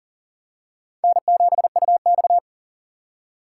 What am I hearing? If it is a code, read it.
Morse code, heard as a keyed tone, about 38 wpm, 700 Hz. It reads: N7UX
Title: USA Call Signs - 4 Characters - 2x Character Spacing - 40wpm